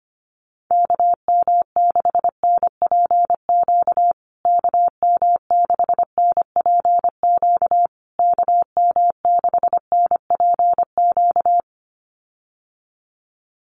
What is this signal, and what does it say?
Morse code, heard as a keyed tone, about 25 wpm, 700 Hz. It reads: KM6NPQ KM6NPQ KM6NPQ
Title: USA Call Signs Repeated 3x 25wpm